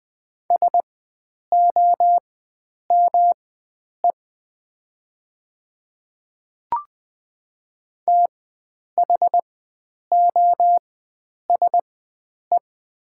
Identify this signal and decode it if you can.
Morse code, heard as a keyed tone, about 20 wpm, spaced as for 9 wpm, 700 Hz. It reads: SOME THOSE
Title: 4x Character Spacing - Top 300 Words - 20wpm